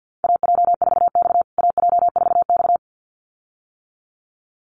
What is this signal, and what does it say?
Morse code, heard as a keyed tone, about 50 wpm, 700 Hz. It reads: AJ4X AJ4X
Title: USA Call Signs Repeated 2x 50wpm